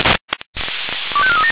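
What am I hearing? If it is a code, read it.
Five-tone selective call (ZVEI), heard as quick runs of five tones, tones of about 70 ms. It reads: 25437
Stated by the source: ZVEI five-tone selective call 25415 received by SDR.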